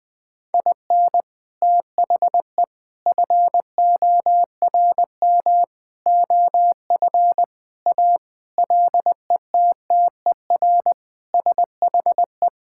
Morse code, heard as a keyed tone, about 21 wpm, 700 Hz. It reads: IN THE FORM OF A LETTER SHE
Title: Sentences with 4-Word Phrases - Spoken Beginning - 20wpm